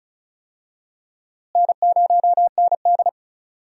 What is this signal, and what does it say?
Morse code, heard as a keyed tone, about 36 wpm, 700 Hz. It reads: N0ND
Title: USA Call Signs - 4 Characters - Repeated 3x - 35wpm